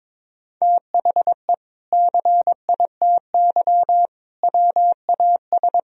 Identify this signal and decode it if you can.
Morse code, heard as a keyed tone, about 22 wpm, 700 Hz. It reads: THE CITY WAS
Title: Sentences from Top 1000 Words - Review Part 3 - 22wpm